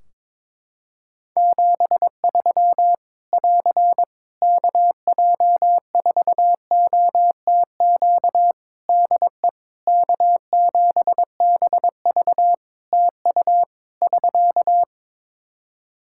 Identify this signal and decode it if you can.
Morse code, heard as a keyed tone, about 22 wpm, 700 Hz. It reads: 73 <AR> KJ4OTQ DE K7BV TU <SK>
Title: K1USN SST QSO Practice 22wpm